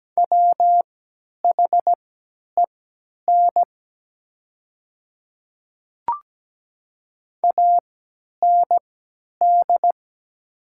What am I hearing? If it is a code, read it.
Morse code, heard as a keyed tone, about 17 wpm, spaced as for 10 wpm, 700 Hz. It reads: WHEN AND
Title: 3x Character Spacing - Top 200 Words - 17wpm